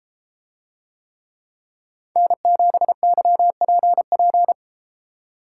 Morse code, heard as a keyed tone, about 32 wpm, 700 Hz. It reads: N7YPP